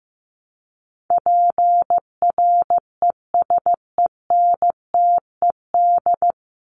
Morse code, heard as a keyed tone, about 15 wpm, 700 Hz. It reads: PRESENTED
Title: ED Suffix - Top 100 Words with Base Words 15wpm